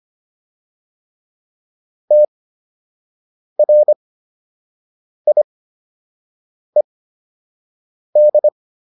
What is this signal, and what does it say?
Morse code, heard as a keyed tone, about 25 wpm, spaced as for 6 wpm, 600 Hz. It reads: TRIED